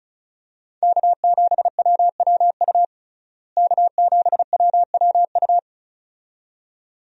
Morse code, heard as a keyed tone, about 34 wpm, 700 Hz. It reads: K7WWU K7WWU
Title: USA Call Signs - 5 Characters - Repeated 2x - 35wpm